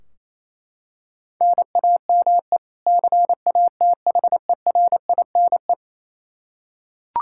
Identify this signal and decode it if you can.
Morse code, heard as a keyed tone, about 28 wpm, 700 Hz. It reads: NAME CATHERINE
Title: Practice QSO - Exchange 1 - Beginning - 28wpm